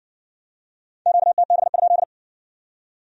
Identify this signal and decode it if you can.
Morse code, heard as a keyed tone, about 56 wpm, 700 Hz. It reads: QTB?